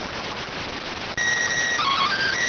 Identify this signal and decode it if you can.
SSTV (slow-scan television), PD 290.